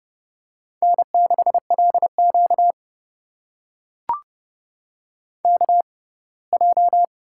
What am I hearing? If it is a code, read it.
Morse code, heard as a keyed tone, about 29 wpm, spaced as for 20 wpm, 700 Hz. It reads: N6LQ KJ
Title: USA Call Signs - 4 Characters - 6x Character Spacing - 30wpm